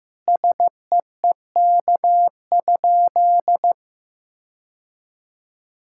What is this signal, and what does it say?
Morse code, heard as a keyed tone, about 15 wpm, 700 Hz. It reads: SEEK?
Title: Sentences from Top 1600 Words 15wpm